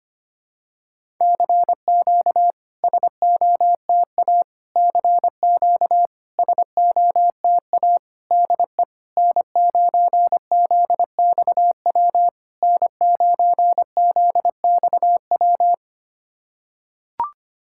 Morse code, heard as a keyed tone, about 25 wpm, 660 Hz. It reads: CQ SOTA CQ SOTA DE N9ZXW N9ZXW
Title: SOTA QSO Practice 25wpm